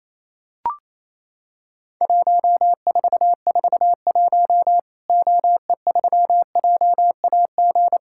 Morse code, heard as a keyed tone, about 27 wpm, 700 Hz. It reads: 1441 OE3JAG
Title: Worked All Europe DX Contest - QTC Exchange - 28wpm